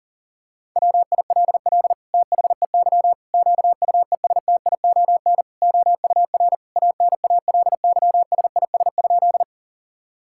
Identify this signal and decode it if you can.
Morse code, heard as a keyed tone, about 40 wpm, 700 Hz. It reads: WILL THEY QUESTION OUR ANALYSIS?
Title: Sentences from Top 1000 Words - Review Part 5 - 40wpm